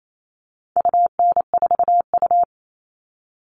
Morse code, heard as a keyed tone, about 28 wpm, 700 Hz. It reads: UN4U
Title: All Cut Numbers - 4-Digit 28wpm